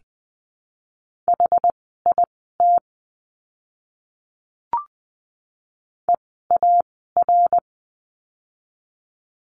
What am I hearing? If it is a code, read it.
Morse code, heard as a keyed tone, about 20 wpm, spaced as for 14 wpm, 700 Hz. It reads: HIT EAR
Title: 068 Lesson 12f - Letter d - Words 20wpm